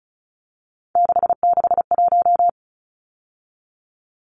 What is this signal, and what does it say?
Morse code, heard as a keyed tone, about 34 wpm, 700 Hz. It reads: 661
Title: Numbers - 3-Digit 35wpm